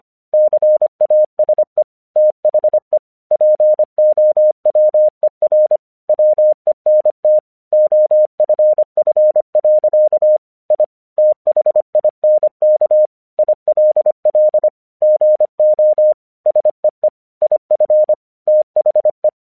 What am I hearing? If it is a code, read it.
Morse code, heard as a keyed tone, about 25 wpm, 610 Hz. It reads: CASE THE POWER WENT OFF. I THINK ILL GO SEE IF THE
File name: 1 English 25wpm / AQC05